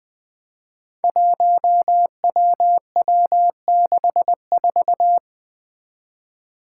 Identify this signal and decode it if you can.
Morse code, heard as a keyed tone, about 20 wpm, 700 Hz. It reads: 1WW64
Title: Five Challenging Letter-Number Combinations 20wpm